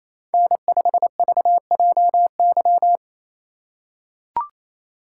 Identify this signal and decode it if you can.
Morse code, heard as a keyed tone, about 28 wpm, 700 Hz. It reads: N5VJY
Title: USA Call Signs - 5 Characters - 4x Character Spacing - 28wpm